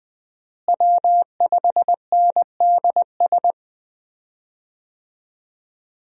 Morse code, heard as a keyed tone, about 20 wpm, 700 Hz. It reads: W5NDS
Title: USA Call Signs - 5 Characters - Repeated 2x - 20wpm